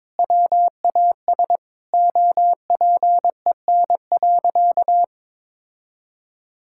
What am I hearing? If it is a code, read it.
Morse code, heard as a keyed tone, about 22 wpm, 700 Hz. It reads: WAS OPEN.
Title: Sentences from Top 1600 Words 22wpm